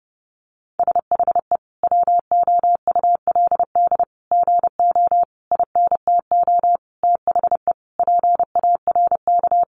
Morse code, heard as a keyed tone, about 29 wpm, 700 Hz. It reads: SHE WOULD GO INTO THE PARK
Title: Sentences from Top 1300 Words 30wpm